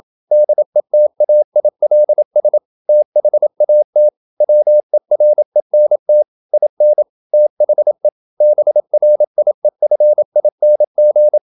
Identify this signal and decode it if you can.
Morse code, heard as a keyed tone, about 28 wpm, 600 Hz. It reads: DETAILS THAT WERENT IN THE BRIEFING
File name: Never 0027wpm